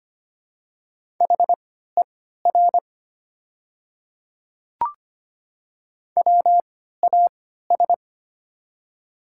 Morse code, heard as a keyed tone, about 25 wpm, spaced as for 14 wpm, 700 Hz. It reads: HER WAS
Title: 3x Character Spacing - Top 100 Words - 25wpm